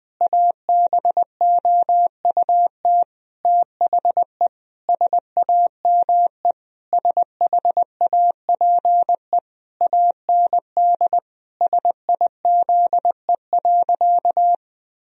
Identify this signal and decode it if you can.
Morse code, heard as a keyed tone, about 20 wpm, 700 Hz. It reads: ABOUT THE SAME SHAPE AND SIZE.
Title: Sentences from Top 1000 Words - Review Part 3 - 20wpm